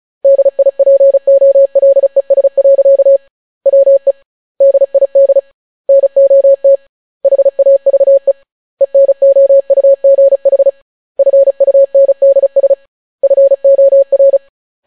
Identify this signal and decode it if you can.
Morse code, heard as a keyed tone, about 36 wpm, 550 Hz. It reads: DIPOLES. WE DID NOT HAVE ENOUGH FUNDS FOR